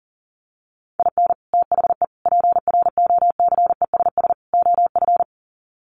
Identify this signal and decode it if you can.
Morse code, heard as a keyed tone, about 37 wpm, 700 Hz. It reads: IN THE PROCESS OF
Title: Sets of 4 Words 40wpm